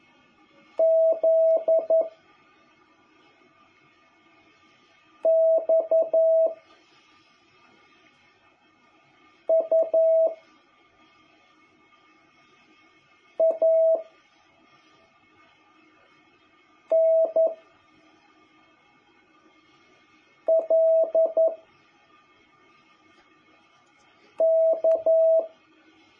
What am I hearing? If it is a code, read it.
Morse code, heard as a keyed tone, about 11 wpm, 650 Hz. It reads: Z X U A N L K
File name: jh8cbh / 2021cw